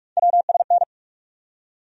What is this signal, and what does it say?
Morse code, heard as a keyed tone, about 46 wpm, 700 Hz. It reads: WSN
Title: USA Call Sign Postfixes - 3 Letters 45wpm